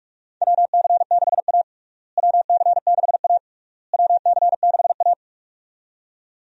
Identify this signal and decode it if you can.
Morse code, heard as a keyed tone, about 46 wpm, 700 Hz. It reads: WC6A WC6A WC6A
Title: USA Call Signs - 4 Characters - Repeated 3x - 45wpm